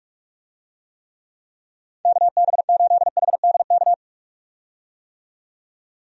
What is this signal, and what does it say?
Morse code, heard as a keyed tone, about 43 wpm, 700 Hz. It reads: KB8HDK
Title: USA Call Signs - 6 Characters - 45wpm